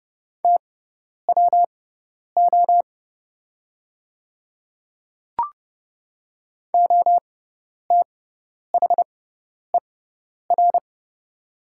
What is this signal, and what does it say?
Morse code, heard as a keyed tone, about 30 wpm, spaced as for 10 wpm, 700 Hz. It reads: TWO OTHER
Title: 6x Character Spacing - Top 100 Words - 30wpm